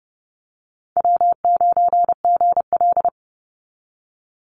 Morse code, heard as a keyed tone, about 30 wpm, 700 Hz. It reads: W9GL
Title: Easy Call Signs Repeated 3x - 30wpm